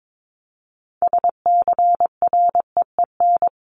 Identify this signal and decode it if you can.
Morse code, heard as a keyed tone, about 22 wpm, 700 Hz. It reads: SCREEN
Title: Top 751-1000 Words 22wpm